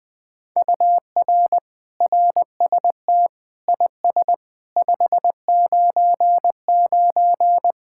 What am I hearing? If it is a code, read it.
Morse code, heard as a keyed tone, about 20 wpm, 700 Hz. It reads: UR RST IS 599
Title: Speed-Racing Common QSO Phrases - 17wpm